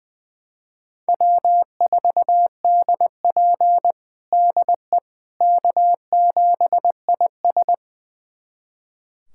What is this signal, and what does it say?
Morse code, heard as a keyed tone, about 20 wpm, 700 Hz. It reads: W4DP DE K7IS
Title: USA Callsign DE Callsign - 4 Characters - 20wpm